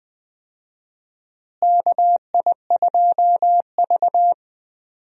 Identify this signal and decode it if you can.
Morse code, heard as a keyed tone, about 20 wpm, 700 Hz. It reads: KI2V
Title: USA Call Signs - 4 Characters - Repeated 3x - 20wpm